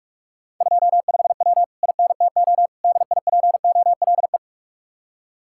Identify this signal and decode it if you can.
Morse code, heard as a keyed tone, about 44 wpm, 700 Hz. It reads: 25W INTO DIPOLE